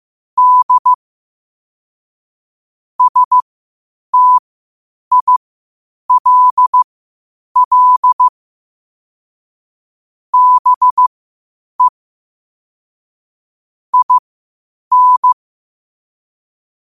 Morse code, heard as a keyed tone, about 15 wpm, spaced as for 8 wpm, 1000 Hz. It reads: D STILL BE IN
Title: 8 WPM morse code quotes for Thu, 29 Feb 2024